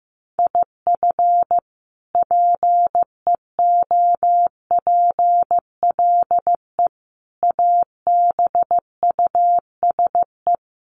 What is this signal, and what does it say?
Morse code, heard as a keyed tone, about 15 wpm, 700 Hz. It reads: IF PEOPLE ABUSE
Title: Sentences from Top 1300 Words 15wpm